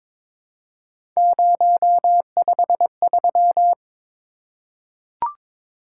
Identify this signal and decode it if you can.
Morse code, heard as a keyed tone, about 22 wpm, 700 Hz. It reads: 053
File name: practice / 22wpm / Three Letter-Number Combinations - 22wpm